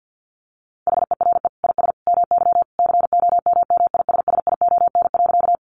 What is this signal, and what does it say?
Morse code, heard as a keyed tone, about 50 wpm, 700 Hz. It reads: HERE IS MY COMMISSION.
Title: Sentences from Top 1400 Words 50wpm